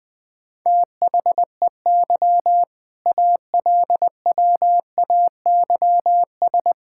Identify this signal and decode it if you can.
Morse code, heard as a keyed tone, about 21 wpm, 700 Hz. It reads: THEY ALWAYS
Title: All-Age Appropriate Jokes - 20wpm